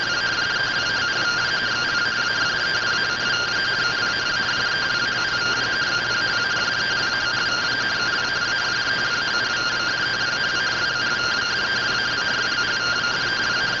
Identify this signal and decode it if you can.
Two-tone teleprinter signal, about 170 Hz shift.
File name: findings / SAGITAL fr15 nb203 3mm.gen-45 rtty rx